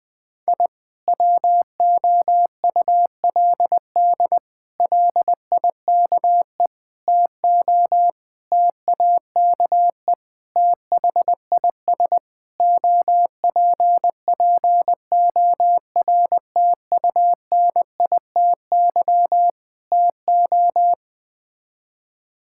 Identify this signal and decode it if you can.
Morse code, heard as a keyed tone, about 21 wpm, 700 Hz. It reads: I WOULD LIKE TO TAKE THIS OPPORTUNITY TO
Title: Sets of 8 Words - 20wpm